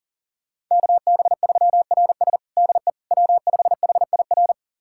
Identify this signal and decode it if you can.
Morse code, heard as a keyed tone, about 37 wpm, 700 Hz. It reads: KB3RS DE W5HIR
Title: USA Callsign DE Callsign 40wpm